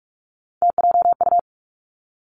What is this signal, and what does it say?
Morse code, heard as a keyed tone, about 46 wpm, 700 Hz. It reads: TJU
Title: Three Challenging Letter-Number Combinations 45wpm